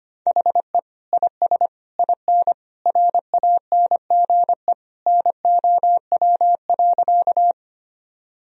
Morse code, heard as a keyed tone, about 25 wpm, 700 Hz. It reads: HE IS IN RANGE NOW.
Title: Sentences from Top 1000 Words - Review Part 4 - 25wpm